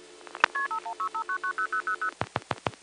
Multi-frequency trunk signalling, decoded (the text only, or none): KP5165990000